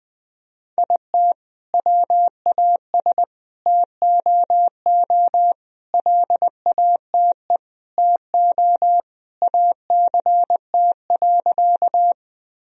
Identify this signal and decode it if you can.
Morse code, heard as a keyed tone, about 20 wpm, 700 Hz. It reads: IT WAS TOO LATE TO ACT.
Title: Sentences from Top 1000 Words - Review Part 1 - 20wpm